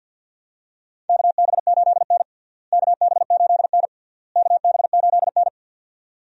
Morse code, heard as a keyed tone, about 51 wpm, 700 Hz. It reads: KB8N KB8N KB8N
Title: USA Call Signs - 4 Characters - Repeated 3x - 50wpm